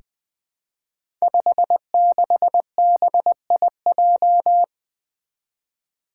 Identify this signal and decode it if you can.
Morse code, heard as a keyed tone, about 20 wpm, 700 Hz. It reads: 56BIJ
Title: Five Challenging Letter-Number Combinations 20wpm